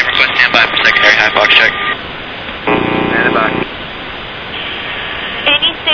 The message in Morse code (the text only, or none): none